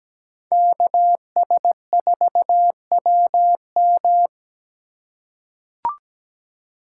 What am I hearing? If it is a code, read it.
Morse code, heard as a keyed tone, about 17 wpm, 700 Hz. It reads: KS4WM
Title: USA Call Signs - 5 Characters - 3x Character Spacing - 17wpm